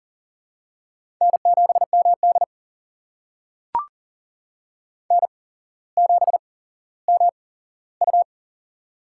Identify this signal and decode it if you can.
Morse code, heard as a keyed tone, about 40 wpm, spaced as for 17 wpm, 700 Hz. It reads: N7MD N7MU
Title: USA Call Signs - 4 Characters - 8x Character Spacing - 40wpm